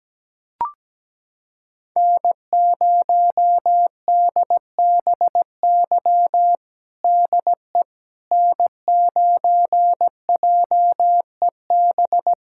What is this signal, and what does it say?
Morse code, heard as a keyed tone, about 17 wpm, 700 Hz. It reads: N0DBY DE N9JEB
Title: USA Callsign DE Callsign 17wpm